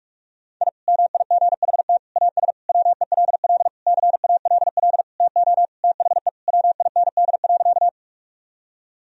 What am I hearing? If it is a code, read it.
Morse code, heard as a keyed tone, about 46 wpm, 700 Hz. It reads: I MIGHT AS WELL CALL TO THE WIND.
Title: Sentences from Top 1000 Words - Review Part 2 - 45wpm